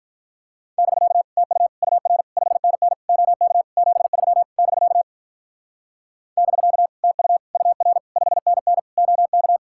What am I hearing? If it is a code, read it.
Morse code, heard as a keyed tone, about 45 wpm, 700 Hz. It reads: <BK> TU UR 5NN OK 73 <BK> <BK> TU UR 5NN OK